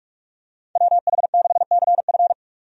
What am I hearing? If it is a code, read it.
Morse code, heard as a keyed tone, about 46 wpm, 700 Hz. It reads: WH6CF
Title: USA Call Signs - 5 Characters - 45wpm